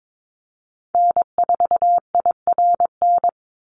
Morse code, heard as a keyed tone, about 22 wpm, 700 Hz. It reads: N4IRN
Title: Call Signs Repeated 3x - 22wpm